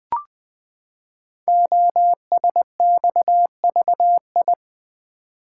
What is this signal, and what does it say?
Morse code, heard as a keyed tone, about 20 wpm, 700 Hz. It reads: OSXVI
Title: Code Groups - 5 Letters - 20wpm